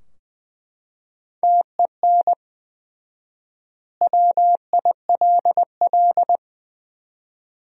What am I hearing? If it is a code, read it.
Morse code, heard as a keyed tone, about 20 wpm, 700 Hz. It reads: TEN WILL
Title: 4x Word-Spacing - Sentences from Top 400 Words - 20wpm